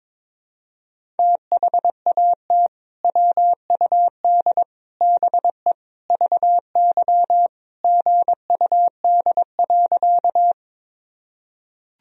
Morse code, heard as a keyed tone, about 22 wpm, 700 Hz. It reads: THAT WUD BE VY GUD.
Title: Sentences from Top 200 Words and Abbreviations 22wpm